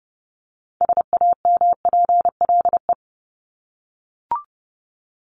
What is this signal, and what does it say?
Morse code, heard as a keyed tone, about 30 wpm, 700 Hz. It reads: SAMPLE
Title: Top 1501-1750 Words 30wpm